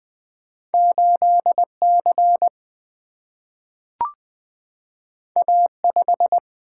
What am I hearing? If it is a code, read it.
Morse code, heard as a keyed tone, about 20 wpm, 700 Hz. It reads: 8 C A 5
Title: Single-Letter Single-Number 20wpm